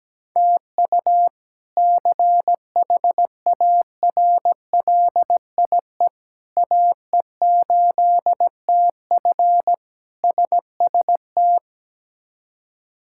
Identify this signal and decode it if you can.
Morse code, heard as a keyed tone, about 17 wpm, 700 Hz. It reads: TU CHARLIE AE8TF SST
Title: K1USN SST QSO Practice - 17wpm